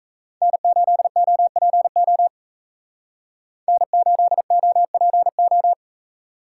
Morse code, heard as a keyed tone, about 44 wpm, 700 Hz. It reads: N8OPO N8OPO